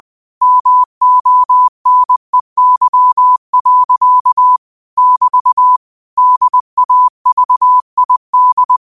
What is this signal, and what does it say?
Morse code, heard as a keyed tone, about 20 wpm, 1000 Hz. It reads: MONEY. <BT> DAVID
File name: qotdfiles / 20wpm